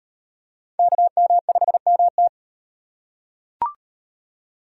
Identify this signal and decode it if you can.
Morse code, heard as a keyed tone, about 38 wpm, 700 Hz. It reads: KM5MT